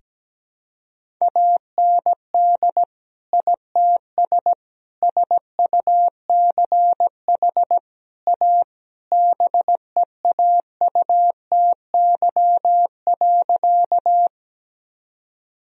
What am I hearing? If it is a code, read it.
Morse code, heard as a keyed tone, about 17 wpm, 700 Hz. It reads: AND ITS SUCH A BEAUTY.
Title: Sentences from Top 1000 Words - Review Part 2 - 17wpm